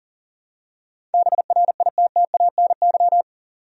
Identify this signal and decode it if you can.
Morse code, heard as a keyed tone, about 40 wpm, 700 Hz. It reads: BRITTANY
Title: USA Common Names - 40wpm